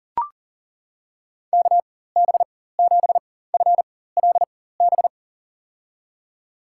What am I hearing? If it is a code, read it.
Morse code, heard as a keyed tone, about 38 wpm, spaced as for 18 wpm, 700 Hz. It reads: KB7FLB
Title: USA Call Signs - 6 Characters - 4x Character Spacing - 40wpm